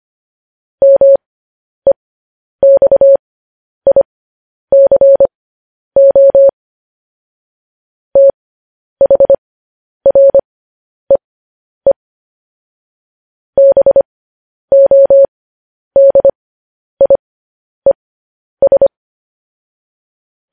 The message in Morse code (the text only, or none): MEXICO THREE BODIES